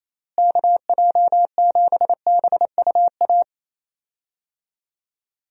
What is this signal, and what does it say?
Morse code, heard as a keyed tone, about 28 wpm, 700 Hz. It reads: KJ7BUA
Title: USA Call Signs - 6 Characters - Repeated 3x - 28wpm